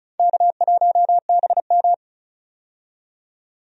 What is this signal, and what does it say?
Morse code, heard as a keyed tone, about 36 wpm, 700 Hz. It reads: K1BM
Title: USA Call Signs - 4 Characters - 35wpm